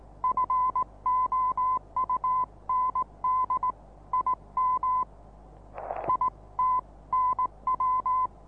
Morse code, heard as a keyed tone, about 19 wpm, 1030 Hz. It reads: FOUND IM IT NW
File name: repos / Morsekurs / DJ4UF Morsen26